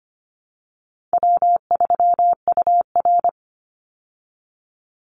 Morse code, heard as a keyed tone, about 25 wpm, 700 Hz. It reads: W3UR
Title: Easy Call Signs Repeated 2x - 25wpm